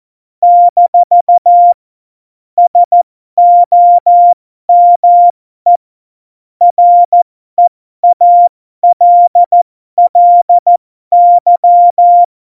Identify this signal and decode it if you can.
Morse code, heard as a keyed tone, about 14 wpm, spaced as for 12 wpm, 700 Hz. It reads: - SOME REALLY
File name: Morsekurs / Witze